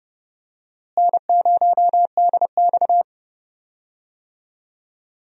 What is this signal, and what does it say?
Morse code, heard as a keyed tone, about 30 wpm, 700 Hz. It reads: N0DX
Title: USA Call Signs - 4 Characters - Repeated 2x - 30wpm